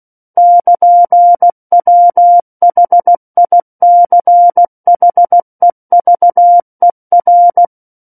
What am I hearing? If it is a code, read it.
Morse code, heard as a keyed tone, about 16 wpm, 700 Hz. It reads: <KN>WHICHEVER
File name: Book One / 07-Chapter 7 - 16 WPM